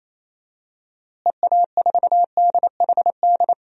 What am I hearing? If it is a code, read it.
Morse code, heard as a keyed tone, about 28 wpm, 700 Hz. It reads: EA4DHD
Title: DXCC-100 Call Signs - 6 Characters - Repeated 2x - 28wpm